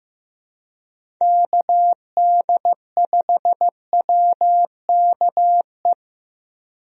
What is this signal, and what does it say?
Morse code, heard as a keyed tone, about 15 wpm, 700 Hz. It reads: KD5WKE
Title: USA Call Signs - 6 Characters - 15wpm